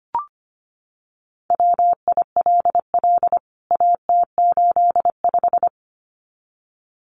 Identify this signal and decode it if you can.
Morse code, heard as a keyed tone, about 25 wpm, 700 Hz. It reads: WILL AT85
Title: CWT Exchange 25wpm